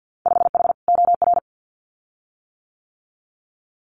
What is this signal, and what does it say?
Morse code, heard as a keyed tone, about 50 wpm, 700 Hz. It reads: 5H OR
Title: Winter Field Day Exchanges Encore 50wpm